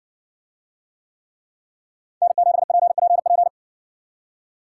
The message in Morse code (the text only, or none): N7PPP